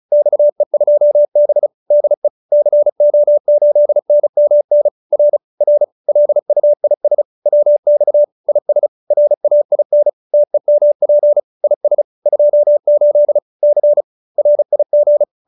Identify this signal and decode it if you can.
Morse code, heard as a keyed tone, about 35 wpm, 590 Hz. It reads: XE2B DE CO8NMN R R LUIS WX IS RAIN TEMP IS 28 C RIG